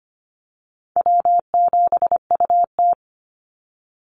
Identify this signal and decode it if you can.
Morse code, heard as a keyed tone, about 25 wpm, 700 Hz. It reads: W7UT
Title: Easy Call Signs Repeated 2x - 25wpm